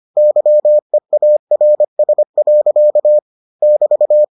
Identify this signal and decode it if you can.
Morse code, heard as a keyed tone, about 25 wpm, 600 Hz. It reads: YEARS. <BT>